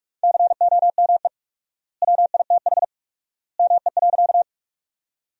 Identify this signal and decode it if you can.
Morse code, heard as a keyed tone, about 44 wpm, 700 Hz. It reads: COME WITH ME.
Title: 4x Word-Spacing - Sentences from Top 200 Words - 45wpm